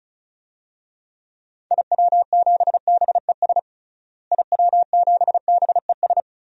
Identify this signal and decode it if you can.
Morse code, heard as a keyed tone, about 36 wpm, 700 Hz. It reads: IW7BES IW7BES
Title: DXCC-100 Call Signs - 6 Characters - Repeated 3x - 35wpm